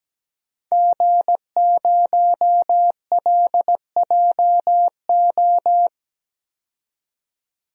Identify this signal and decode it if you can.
Morse code, heard as a keyed tone, about 17 wpm, 700 Hz. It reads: G0LJO
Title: DXCC-100 Call Signs - 5 Characters - 17wpm